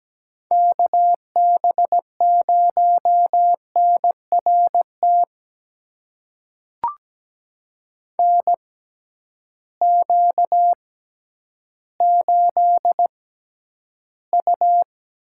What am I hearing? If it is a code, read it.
Morse code, heard as a keyed tone, about 17 wpm, spaced as for 10 wpm, 700 Hz. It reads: KB0NRT NQ8U
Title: USA Call Signs - 6x Character Spacing - 17wpm